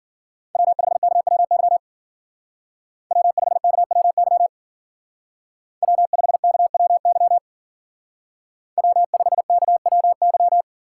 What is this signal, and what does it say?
Morse code, heard as a keyed tone, about 48 wpm, 690 Hz. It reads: W5KWY W5KWY W5KWY W5KWY
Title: Speed Racing - USA Call Signs 40wpm